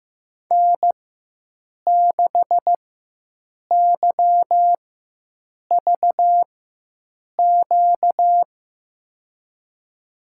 Morse code, heard as a keyed tone, about 15 wpm, spaced as for 7 wpm, 700 Hz. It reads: N6YVQ
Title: USA Call Signs - 5 Characters - 4x Character Spacing - 15wpm